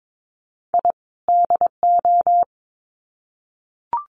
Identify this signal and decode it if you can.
Morse code, heard as a keyed tone, about 22 wpm, 700 Hz. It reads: I DO
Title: Sets of 2 Words 22wpm